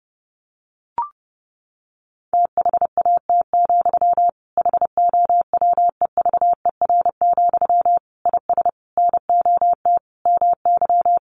Morse code, heard as a keyed tone, about 30 wpm, 700 Hz. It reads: THAT, HOWEVER, IS NOT MY
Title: Sentences from Top 1500 Words 30wpm